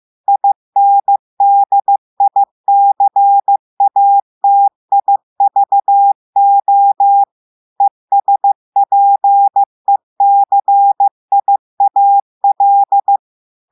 Morse code, heard as a keyed tone, about 15 wpm, 800 Hz. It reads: INDICATIVO ESPECIAL
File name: eh1ard-amateur-radio-day-2016 mf feed 1